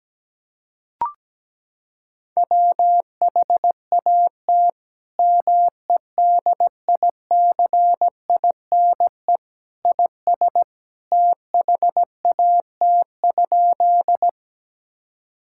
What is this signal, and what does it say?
Morse code, heard as a keyed tone, about 17 wpm, 700 Hz. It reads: WHAT MEDICINE IS THAT?
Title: Sentences from Top 1000 Words - Review Part 6 - 17wpm